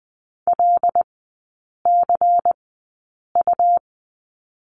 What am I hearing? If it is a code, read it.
Morse code, heard as a keyed tone, about 20 wpm, spaced as for 8 wpm, 700 Hz. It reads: LCU
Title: 107 Lesson 18e - Letter m - Three Characters 20wpm